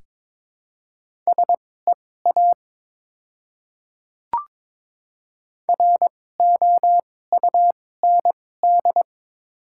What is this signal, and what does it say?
Morse code, heard as a keyed tone, about 21 wpm, spaced as for 16 wpm, 700 Hz. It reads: SEA ROUND
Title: 2x Character Spacing - Top 300 Words - 22wpm